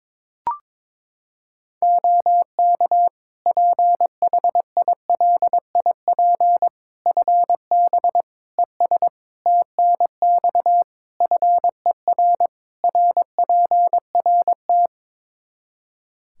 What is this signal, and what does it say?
Morse code, heard as a keyed tone, about 21 wpm, 700 Hz. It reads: OK PHILIP FB ES TNX FER RPRT
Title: K1USN SST QSO Practice 22wpm